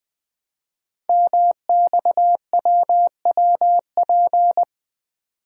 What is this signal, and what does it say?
Morse code, heard as a keyed tone, about 20 wpm, 700 Hz. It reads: MXWWP
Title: Code Groups - 5 Letters - 20wpm